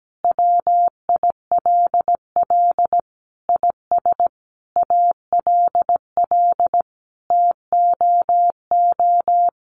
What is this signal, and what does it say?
Morse code, heard as a keyed tone, about 17 wpm, 700 Hz. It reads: WILL IS ALL TOO
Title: Sentences from Top 1500 Words 17wpm